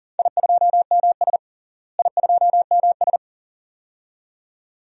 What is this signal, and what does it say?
Morse code, heard as a keyed tone, about 40 wpm, 700 Hz. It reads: I2MS I2MS
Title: DXCC-100 Call Signs - 4 Characters - Repeated 2x - 40wpm